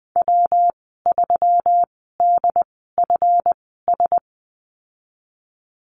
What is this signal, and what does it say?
Morse code, heard as a keyed tone, about 20 wpm, spaced as for 15 wpm, 700 Hz. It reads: W3DFS
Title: 161 Lesson 26g - Letter g - Call Signs 20wpm